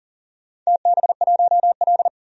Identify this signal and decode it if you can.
Morse code, heard as a keyed tone, about 39 wpm, 700 Hz. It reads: TB1L
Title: DXCC-100 Call Signs - 4 Characters - 40wpm